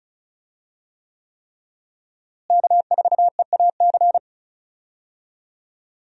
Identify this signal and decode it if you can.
Morse code, heard as a keyed tone, about 36 wpm, 700 Hz. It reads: K4EAC
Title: USA Call Signs - 5 Characters - Repeated 3x - 35wpm